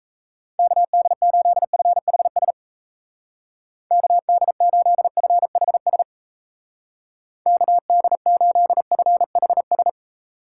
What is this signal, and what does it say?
Morse code, heard as a keyed tone, about 35 wpm, 700 Hz. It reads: KD8FHS KD8FHS KD8FHS